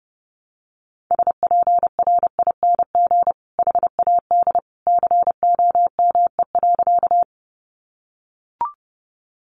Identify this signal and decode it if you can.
Morse code, heard as a keyed tone, about 30 wpm, 700 Hz. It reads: SPRING HAD COME.
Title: Sentences from Top 1200 Words 30wpm